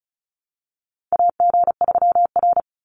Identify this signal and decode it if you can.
Morse code, heard as a keyed tone, about 34 wpm, 700 Hz. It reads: AG3R
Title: Call Signs Repeated 3x - 35wpm